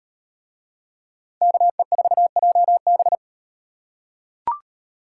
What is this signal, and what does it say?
Morse code, heard as a keyed tone, about 36 wpm, 700 Hz. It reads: KE4JB